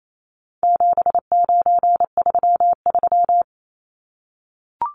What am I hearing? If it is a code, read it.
Morse code, heard as a keyed tone, about 28 wpm, 700 Hz. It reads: 7933
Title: Numbers - 4-Digit 28wpm